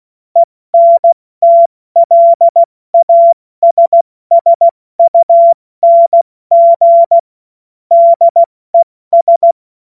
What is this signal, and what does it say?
Morse code, heard as a keyed tone, about 16 wpm, spaced as for 14 wpm, 680 Hz. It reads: ENTLASSUNG DES
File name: Morsekurs / Witze / Witz 05 Dezember 2023 70 F80